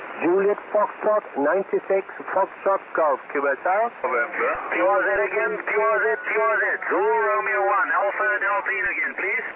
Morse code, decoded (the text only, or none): none